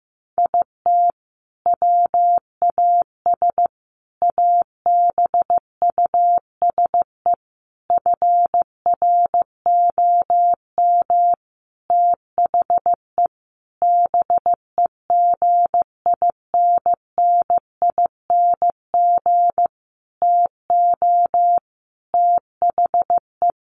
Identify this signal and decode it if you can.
Morse code, heard as a keyed tone, about 15 wpm, 700 Hz. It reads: IT WAS ABUSE FROM THE BEGINNING TO THE
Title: Sentences from Top 1300 Words 15wpm